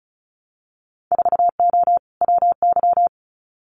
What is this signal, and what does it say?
Morse code, heard as a keyed tone, about 36 wpm, 700 Hz. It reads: 4O WY
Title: Winter Field Day Exchanges Encore 35wpm